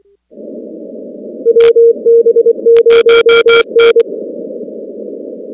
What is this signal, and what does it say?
Morse code, heard as a keyed tone, about 24 wpm, 460 Hz. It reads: WB0N
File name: contests / 2019-cw-wpx-kt5j-k5pi- -2330 0-19